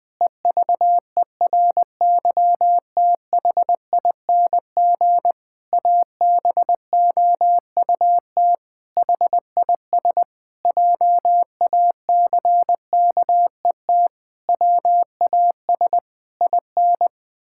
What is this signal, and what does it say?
Morse code, heard as a keyed tone, about 21 wpm, 700 Hz. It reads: EVERYTHING ABOUT HIS JACKET WAS IN
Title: Pangram Challenge v2 - 20wpm